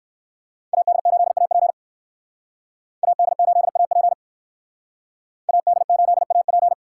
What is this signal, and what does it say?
Morse code, heard as a keyed tone, about 58 wpm, 700 Hz. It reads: AD8AP AD8AP AD8AP